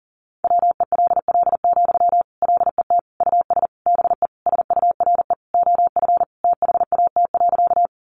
Morse code, heard as a keyed tone, about 40 wpm, 700 Hz. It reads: WELL, LET US BE SURE OF THAT.
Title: Sentences from Top 300 Words 40wpm